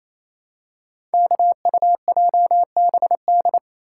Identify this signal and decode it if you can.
Morse code, heard as a keyed tone, about 29 wpm, 700 Hz. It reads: KUJBD
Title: Code Groups - 5 Letter:Numbers - 28wpm